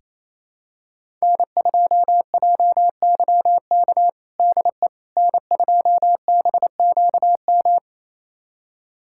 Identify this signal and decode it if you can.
Morse code, heard as a keyed tone, about 28 wpm, 700 Hz. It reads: N2JYK DE N2BQM
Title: USA Callsign DE Callsign - 5 Characters - 28wpm